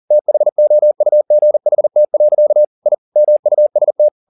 Morse code, heard as a keyed tone, about 40 wpm, 600 Hz. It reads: THOUGHT. I MUST